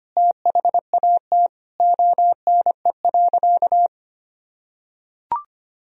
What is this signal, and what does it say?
Morse code, heard as a keyed tone, about 25 wpm, 700 Hz. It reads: THAT ONE.
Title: Sentences from Top 1000 Words - Review Part 4 - 25wpm